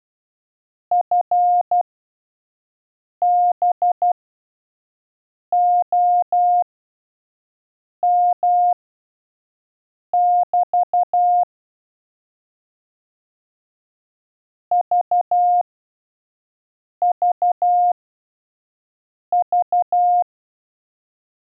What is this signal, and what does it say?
Morse code, heard as a keyed tone, about 12 wpm, spaced as for 5 wpm, 710 Hz. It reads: FB OM <BT> VVV
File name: 5WPM test 1